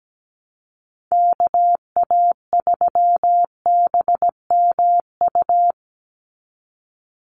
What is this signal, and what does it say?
Morse code, heard as a keyed tone, about 17 wpm, 700 Hz. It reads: KA3BMU
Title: Call Signs Repeated 2x - 17wpm